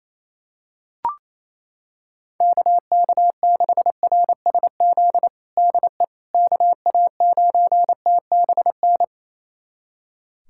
Morse code, heard as a keyed tone, about 28 wpm, 700 Hz. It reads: KK6RSZ DE KA9TBN
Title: USA Callsign DE Callsign 28wpm